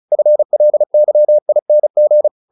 Morse code, heard as a keyed tone, about 34 wpm, 600 Hz. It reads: FLYING